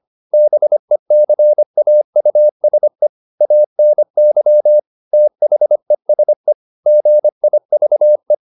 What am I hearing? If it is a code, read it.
Morse code, heard as a keyed tone, about 25 wpm, 600 Hz. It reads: BECAUSE ANY THESE GIVE
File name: Top-100-Words-25